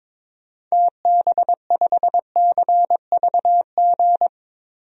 Morse code, heard as a keyed tone, about 22 wpm, 700 Hz. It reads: TB5CVG
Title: DXCC-100 Call Signs - 6 Characters - 22wpm